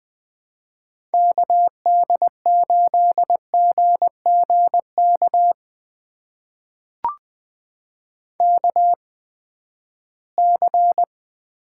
Morse code, heard as a keyed tone, about 20 wpm, spaced as for 14 wpm, 700 Hz. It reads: KD8GGK KC
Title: USA Call Signs - 6 Characters - 8x Character Spacing - 20wpm